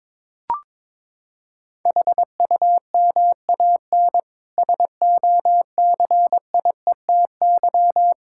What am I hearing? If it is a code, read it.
Morse code, heard as a keyed tone, about 21 wpm, 700 Hz. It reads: HUMAN SOCIETY
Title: Sentences from Top 1000 Words - Review Part 6 - 22wpm